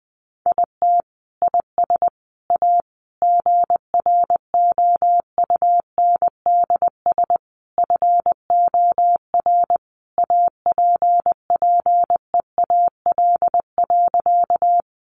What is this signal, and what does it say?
Morse code, heard as a keyed tone, about 19 wpm, 700 Hz. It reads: IT IS A GROUNDS FOR APPEAL.
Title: Sentences from Top 1300 Words 20wpm